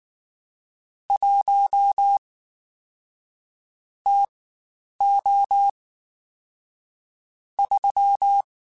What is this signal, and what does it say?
Morse code, heard as a keyed tone, about 19 wpm, spaced as for 9 wpm, 800 Hz. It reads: 1 TO 3
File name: Spaced Characters / Spaced Characters-mp3s / Weather 5 1 sp @18